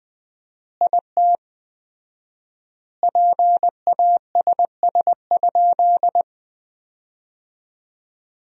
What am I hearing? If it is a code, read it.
Morse code, heard as a keyed tone, about 20 wpm, 700 Hz. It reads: IT PASS?
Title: 4x Word-Spacing - Sentences from Top 400 Words - 20wpm